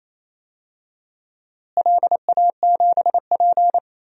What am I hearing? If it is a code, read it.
Morse code, heard as a keyed tone, about 28 wpm, 700 Hz. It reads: LA7P
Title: DXCC-100 Call Signs - 4 Characters - Repeated 2x - 28wpm